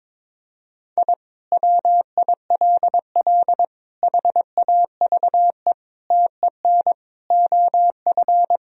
Morse code, heard as a keyed tone, about 22 wpm, 700 Hz. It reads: I WILL HAVE TEN OF
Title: Sentences from Top 1000 Words - Review Part 2 - 22wpm